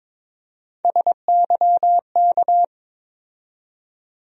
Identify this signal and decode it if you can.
Morse code, heard as a keyed tone, about 22 wpm, 700 Hz. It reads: SYK
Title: USA Call Sign Postfixes - 3 Letters 22wpm